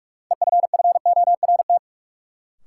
Morse code, heard as a keyed tone, about 46 wpm, 700 Hz. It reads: EFFORT
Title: Sentences from Top 1200 Words - New Word Drill Alt - 45wpm